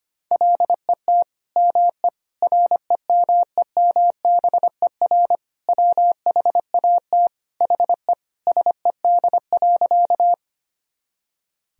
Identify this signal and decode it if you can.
Morse code, heard as a keyed tone, about 25 wpm, 700 Hz. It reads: LET ME REMEMBER WHAT HE SED.
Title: Sentences from Top 400 Words and Abbreviations 25wpm